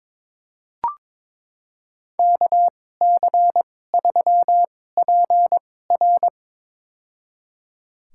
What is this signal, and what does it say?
Morse code, heard as a keyed tone, about 22 wpm, spaced as for 16 wpm, 700 Hz. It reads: KC3PR
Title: USA Call Signs - 5 Characters - 2x Character Spacing - 22wpm